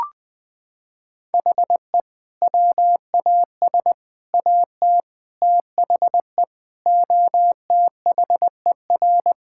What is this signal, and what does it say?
Morse code, heard as a keyed tone, about 20 wpm, 700 Hz. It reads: HE WAS AT THE OTHER
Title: Sentences with 5-Word Phrases - 20wpm